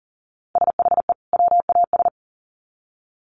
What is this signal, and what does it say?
Morse code, heard as a keyed tone, about 40 wpm, 700 Hz. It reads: SHE WAS
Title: Sets of 2 Words 40wpm